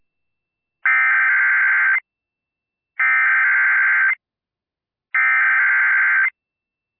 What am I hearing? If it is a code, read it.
Emergency Alert System (SAME) tones (header bursts), heard as harsh buzzing bursts. It is heard from 0.9 to 6.3 s.